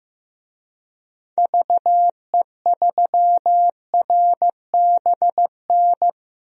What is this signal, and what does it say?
Morse code, heard as a keyed tone, about 15 wpm, 700 Hz. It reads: VE3RBN
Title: DXCC-100 Call Signs - 6 Characters - 15wpm